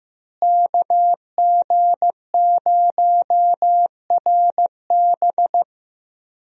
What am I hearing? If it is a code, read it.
Morse code, heard as a keyed tone, about 15 wpm, 700 Hz. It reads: KG0RB
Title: USA Call Signs - 5 Characters - Repeated 3x - 15wpm